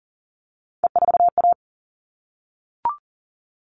Morse code, heard as a keyed tone, about 40 wpm, 700 Hz. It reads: E4A
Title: All Cut Numbers - 3-Digit 40wpm